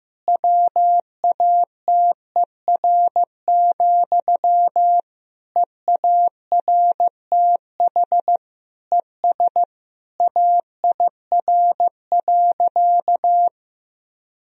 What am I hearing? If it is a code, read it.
Morse code, heard as a keyed tone, about 15 wpm, 700 Hz. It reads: WATER, EARTH ES AIR.
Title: Sentences from Top 400 Words and Abbreviations 15wpm